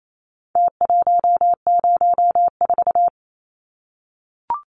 Morse code, heard as a keyed tone, about 28 wpm, 700 Hz. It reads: T104
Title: Numbers - Contest Style - 4-Digit 28wpm